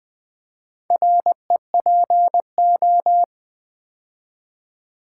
Morse code, heard as a keyed tone, about 20 wpm, 700 Hz. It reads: REPO
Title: Build A Word - 2 Letters At A Time - 6-Letter Words - 20wpm